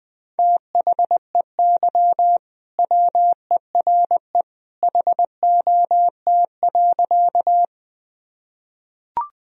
Morse code, heard as a keyed tone, about 19 wpm, 700 Hz. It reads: THEY WERE HOT.
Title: Sentences from Top 1000 Words - Review Part 3 - 20wpm